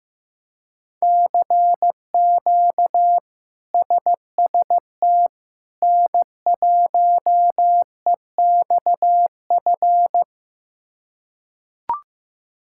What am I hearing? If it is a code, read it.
Morse code, heard as a keyed tone, about 15 wpm, 700 Hz. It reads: CQ SST N1EXF
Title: K1USN SST QSO Practice - 15wpm